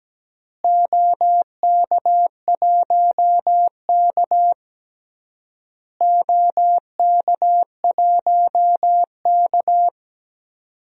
Morse code, heard as a keyed tone, about 17 wpm, 700 Hz. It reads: OK1K OK1K
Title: DXCC-100 Call Signs - 4 Characters - Repeated 3x - 17wpm